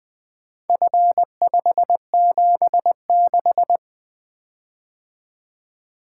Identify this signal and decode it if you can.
Morse code, heard as a keyed tone, about 20 wpm, 700 Hz. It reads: F576
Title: Four Letter-Number Combinations - 20wpm